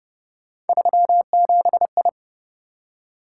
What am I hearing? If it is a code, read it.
Morse code, heard as a keyed tone, about 27 wpm, 700 Hz. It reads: 37I